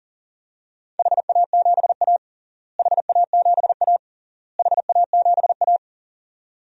Morse code, heard as a keyed tone, about 40 wpm, 700 Hz. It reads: HA7A HA7A HA7A
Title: DXCC-100 Call Signs - 4 Characters - Repeated 3x - 40wpm